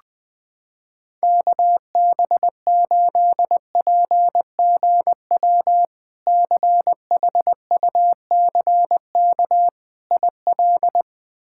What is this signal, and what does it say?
Morse code, heard as a keyed tone, about 20 wpm, 700 Hz. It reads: KB8PGW CHUCK IL
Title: K1USN SST QSO Practice - 20wpm